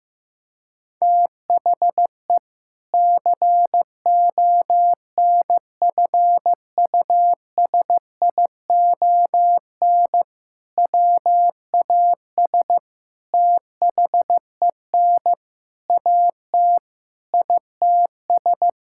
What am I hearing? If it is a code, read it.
Morse code, heard as a keyed tone, about 15 wpm, 700 Hz. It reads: THE CONFUSION WAS THEN AT ITS
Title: Sentences from Top 1600 Words 15wpm